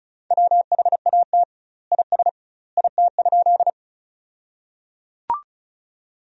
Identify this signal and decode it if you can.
Morse code, heard as a keyed tone, about 36 wpm, 700 Hz. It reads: WHAT IS IT?
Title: 2x Word-Spacing - Sentences from Top 100 Words - 35wpm